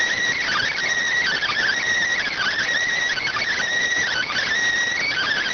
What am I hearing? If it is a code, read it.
Emergency Alert System (SAME) tones, heard as harsh buzzing bursts. It is heard from 0.4 to 4.4 s.